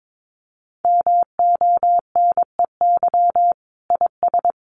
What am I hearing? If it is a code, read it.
Morse code, heard as a keyed tone, about 22 wpm, 700 Hz. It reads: MONEY IS
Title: Sentences from Top 500 Words 22wpm